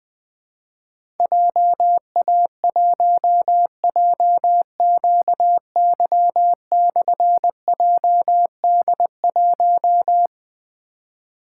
Morse code, heard as a keyed tone, about 20 wpm, 700 Hz. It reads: JA1JQY/JD1
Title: Difficult Call Signs 20wpm